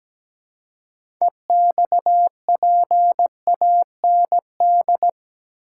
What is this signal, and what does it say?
Morse code, heard as a keyed tone, about 17 wpm, 700 Hz. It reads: EXPAND
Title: Six Letter Words - 17wpm